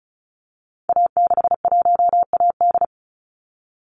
Morse code, heard as a keyed tone, about 34 wpm, 700 Hz. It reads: A61AD
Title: Call Signs Repeated 3x - 35wpm